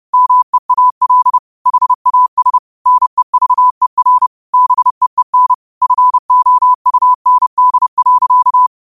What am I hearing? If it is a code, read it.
Morse code, heard as a keyed tone, about 30 wpm, 1000 Hz. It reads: MEAL HAS NEVER BEEN FOUND.
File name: qotdfiles / 30wpm